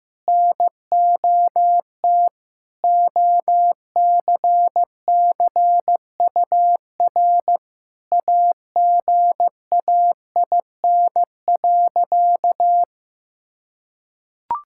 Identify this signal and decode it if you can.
Morse code, heard as a keyed tone, about 16 wpm, 700 Hz. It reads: NOT OCCUR AGAIN.
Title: Sentences from Top 1000 Words - Review Part 6 - 15wpm